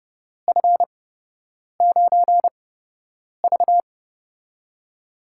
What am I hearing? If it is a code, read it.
Morse code, heard as a keyed tone, about 30 wpm, spaced as for 8 wpm, 700 Hz. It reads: F9V
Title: USA Call Signs - 8x Character Spacing - 30wpm